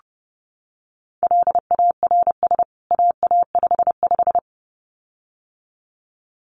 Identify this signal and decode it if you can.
Morse code, heard as a keyed tone, about 30 wpm, 700 Hz. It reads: LARS AA55
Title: CWT Exchange 30wpm